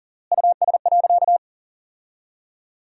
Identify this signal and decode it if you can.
Morse code, heard as a keyed tone, about 40 wpm, 700 Hz. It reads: US.
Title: 3x Word-Spacing - Sentences from Top 500 Words - 40wpm